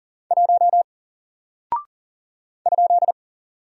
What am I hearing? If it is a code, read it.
Morse code, heard as a keyed tone, about 40 wpm, 700 Hz. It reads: 1 ?
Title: Single Character (Letter:Number:Punctuation:Prosign) Alt 40wpm